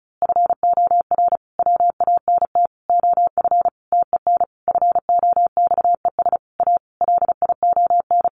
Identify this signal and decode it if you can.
Morse code, heard as a keyed tone, about 34 wpm, 700 Hz. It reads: FOR WANT OF TEN FOXES A LION
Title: A Sound Of Thunder 35wpm